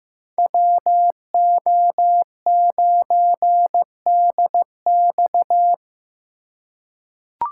Morse code, heard as a keyed tone, about 15 wpm, 700 Hz. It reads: WO9DX
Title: USA Call Signs - 5 Characters - 8x Character Spacing - 15wpm